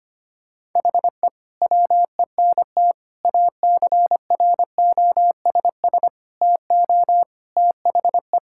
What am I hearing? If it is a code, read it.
Morse code, heard as a keyed tone, about 25 wpm, 700 Hz. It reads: HE WENT ACROSS TO THE
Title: Sentences from Top 1000 Words - Review Part 5 - 25wpm